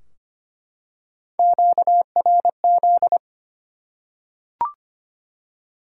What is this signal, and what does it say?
Morse code, heard as a keyed tone, about 25 wpm, 700 Hz. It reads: QRZ
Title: Top 90 QSO Elements 25wpm